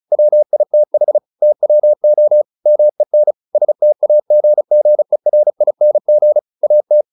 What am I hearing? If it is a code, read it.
Morse code, heard as a keyed tone, about 34 wpm, 600 Hz. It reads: WITH TWO MEN STAGGERING AT
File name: the old man and the sea / the old man and the sea 1 35wpm